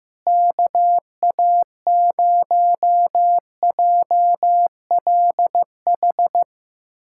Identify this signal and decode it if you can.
Morse code, heard as a keyed tone, about 15 wpm, 700 Hz. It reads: KA0JLH
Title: USA Callsign DE Callsign - 6 Characters - 15wpm